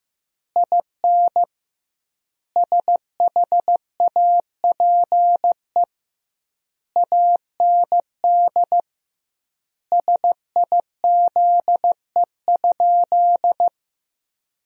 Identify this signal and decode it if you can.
Morse code, heard as a keyed tone, about 15 wpm, 700 Hz. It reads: IN SHAPE AND SIZE?
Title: 2x Word-Spacing - Sentences from Top 500 Words - 15wpm